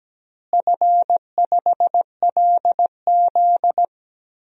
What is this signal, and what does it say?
Morse code, heard as a keyed tone, about 17 wpm, 700 Hz. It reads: F5LZ
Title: DXCC-100 Call Signs - 4 Characters - Repeated 2x - 17wpm